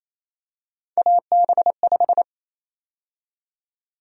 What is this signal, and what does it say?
Morse code, heard as a keyed tone, about 28 wpm, 700 Hz. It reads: AB5
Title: USA Call Sign Prefixes - 28wpm